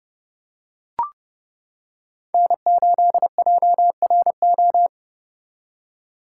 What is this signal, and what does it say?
Morse code, heard as a keyed tone, about 29 wpm, 700 Hz. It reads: N8JRO